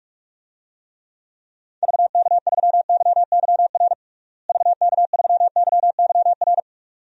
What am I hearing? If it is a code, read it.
Morse code, heard as a keyed tone, about 45 wpm, 700 Hz. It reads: VK3YYR VK3YYR
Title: DXCC-100 Call Signs - 6 Characters - Repeated 2x - 45wpm